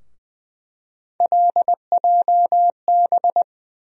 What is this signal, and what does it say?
Morse code, heard as a keyed tone, about 20 wpm, 700 Hz. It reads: LJB
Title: USA Call Sign Postfixes - 3 Letters 20wpm